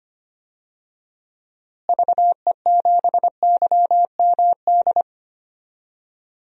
Morse code, heard as a keyed tone, about 25 wpm, 700 Hz. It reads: VE7YMD
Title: DXCC-100 Call Signs - 6 Characters - Repeated 2x - 25wpm